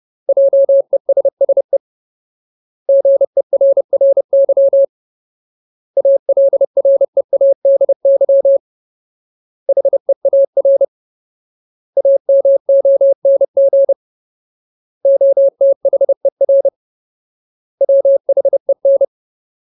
Morse code, heard as a keyed tone, about 30 wpm, 550 Hz. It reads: JESSE GERRY ALREADY HEAR AMONG OTHER WHEN
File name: Ultimate Word List / 30WPM Ultimate